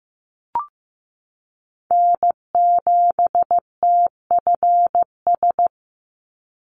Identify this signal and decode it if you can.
Morse code, heard as a keyed tone, about 15 wpm, 700 Hz. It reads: N7TFS
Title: USA Call Signs 15wpm